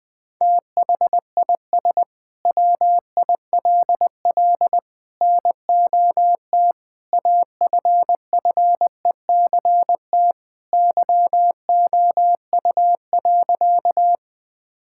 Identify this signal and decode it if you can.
Morse code, heard as a keyed tone, about 21 wpm, 700 Hz. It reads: THIS WILL NOT AFFECT YOU.
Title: Sentences from Top 1000 Words - Review Part 6 - 20wpm